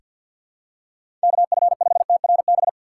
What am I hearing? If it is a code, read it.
Morse code, heard as a keyed tone, about 48 wpm, 700 Hz. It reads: KF5TRB
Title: USA Call Signs - 6 Characters - 6x Character Spacing - 50wpm